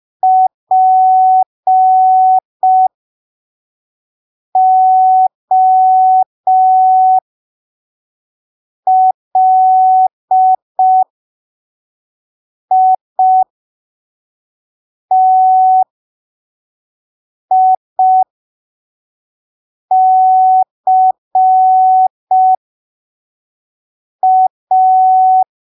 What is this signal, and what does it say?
Morse code, heard as a keyed tone, about 5 wpm, spaced as for 3 wpm, 750 Hz. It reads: POLITICA
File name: arquivos / texto / texto 24 05ppm